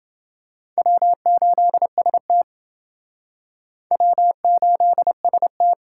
Morse code, heard as a keyed tone, about 29 wpm, 700 Hz. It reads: W8ST W8ST